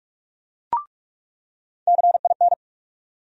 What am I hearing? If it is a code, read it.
Morse code, heard as a keyed tone, about 46 wpm, 700 Hz. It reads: CIN
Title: USA Call Sign Postfixes - 3 Letters 45wpm